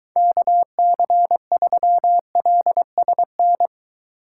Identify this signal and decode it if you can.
Morse code, heard as a keyed tone, about 23 wpm, 700 Hz. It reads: KC3LSN